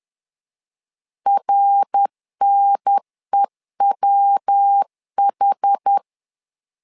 Morse code, heard as a keyed tone, about 10 wpm, 800 Hz. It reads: RNEWH
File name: LEZIONI / 4 Q1S3 10wpm 2427